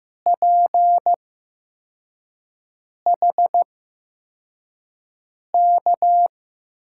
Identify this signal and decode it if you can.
Morse code, heard as a keyed tone, about 15 wpm, spaced as for 4 wpm, 700 Hz. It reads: PHK
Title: USA Call Signs - 8x Character Spacing - 15wpm